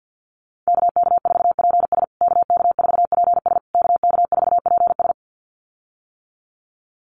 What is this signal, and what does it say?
Morse code, heard as a keyed tone, about 50 wpm, 700 Hz. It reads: KK4PS KK4PS KK4PS
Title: USA Call Signs Repeated 3x 50wpm